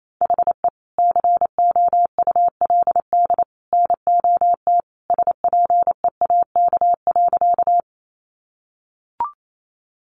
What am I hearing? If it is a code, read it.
Morse code, heard as a keyed tone, about 28 wpm, 700 Hz. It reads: HE COULD NOT SPEAK.
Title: Sentences from Top 1100 Words 28wpm